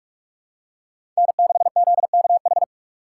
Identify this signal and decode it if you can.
Morse code, heard as a keyed tone, about 42 wpm, 700 Hz. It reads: N6ZKH
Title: USA Call Signs - 5 Characters - 6x Character Spacing - 45wpm